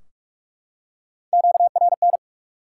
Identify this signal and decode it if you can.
Morse code, heard as a keyed tone, about 46 wpm, 700 Hz. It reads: QRN
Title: Top 90 QSO Elements 45wpm